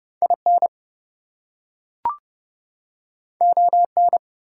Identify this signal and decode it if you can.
Morse code, heard as a keyed tone, about 30 wpm, 700 Hz. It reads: IN ON
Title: Two Letter Words - 30wpm